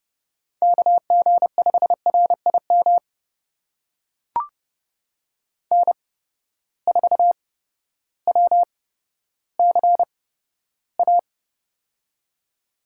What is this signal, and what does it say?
Morse code, heard as a keyed tone, about 30 wpm, spaced as for 14 wpm, 700 Hz. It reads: KG5RIM N4WCA
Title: USA Call Signs - 8x Character Spacing - 30wpm